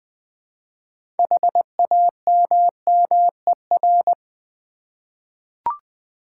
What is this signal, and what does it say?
Morse code, heard as a keyed tone, about 20 wpm, 700 Hz. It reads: HAMMER
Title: Easy Words with Hints - 20wpm